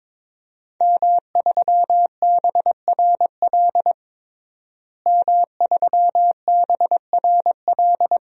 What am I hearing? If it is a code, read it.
Morse code, heard as a keyed tone, about 22 wpm, 700 Hz. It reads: M3BRL M3BRL
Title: DXCC-100 Call Signs - 5 Characters - Repeated 3x - 22wpm